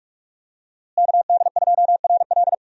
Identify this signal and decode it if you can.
Morse code, heard as a keyed tone, about 42 wpm, 700 Hz. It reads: KD2RL
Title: USA Call Signs - 5 Characters - Repeated 2x - 45wpm